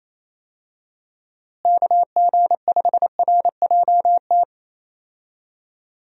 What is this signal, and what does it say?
Morse code, heard as a keyed tone, about 28 wpm, 700 Hz. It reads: KG5RJT
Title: USA Call Signs - 6 Characters - Repeated 2x - 28wpm